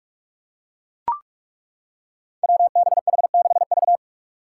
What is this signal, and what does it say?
Morse code, heard as a keyed tone, about 42 wpm, 700 Hz. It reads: WBH6V
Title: Five Challenging Letter-Number Combinations 45wpm